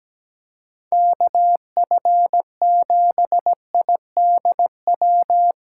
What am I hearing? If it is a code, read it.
Morse code, heard as a keyed tone, about 17 wpm, 700 Hz. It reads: KF7IDW
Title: USA Callsign DE Callsign - 6 Characters - 17wpm